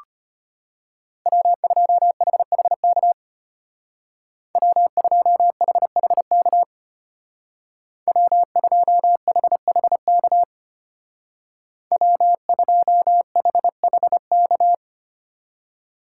Morse code, heard as a keyed tone, about 33 wpm, 700 Hz. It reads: W2HHK W2HHK W2HHK W2HHK